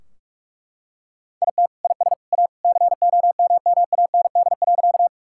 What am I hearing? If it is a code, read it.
Morse code, heard as a keyed tone, about 41 wpm, 700 Hz. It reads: IT IS A COMMAND.
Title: Sentences from Top 1600 Words 45wpm